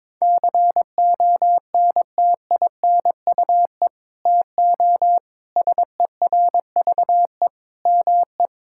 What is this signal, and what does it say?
Morse code, heard as a keyed tone, about 21 wpm, 700 Hz. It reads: CONTINUE TO SERVE ME
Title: Sentences from Top 1000 Words - Review Part 6 - 22wpm